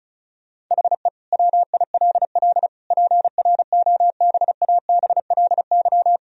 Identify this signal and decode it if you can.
Morse code, heard as a keyed tone, about 36 wpm, 700 Hz. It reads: HE WILL PROBABLY
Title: Sentences from Top 1000 Words - Review Part 4 - 35wpm